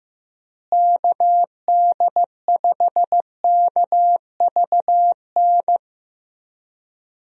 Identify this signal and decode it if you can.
Morse code, heard as a keyed tone, about 16 wpm, 700 Hz. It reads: KD5KVN
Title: USA Call Signs - 6 Characters - Repeated 3x - 15wpm